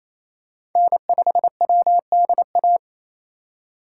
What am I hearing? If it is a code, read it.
Morse code, heard as a keyed tone, about 28 wpm, 700 Hz. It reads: N5WDA
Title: USA Call Signs - 5 Characters - 8x Character Spacing - 28wpm